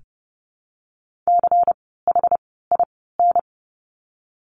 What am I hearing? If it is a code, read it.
Morse code, heard as a keyed tone, about 30 wpm, 700 Hz. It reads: C H I N
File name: Lesson 16f - Letter c - Words 30wpm